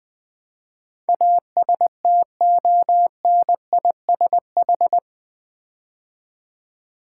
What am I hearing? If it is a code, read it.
Morse code, heard as a keyed tone, about 20 wpm, 700 Hz. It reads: ASTONISH
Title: Build A Word - 4 Letters At A Time - 11-Letter Words - 20wpm